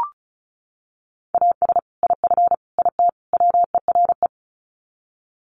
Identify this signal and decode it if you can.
Morse code, heard as a keyed tone, about 34 wpm, 700 Hz. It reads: AS IF IT WERE
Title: Sets of 4 Words 35wpm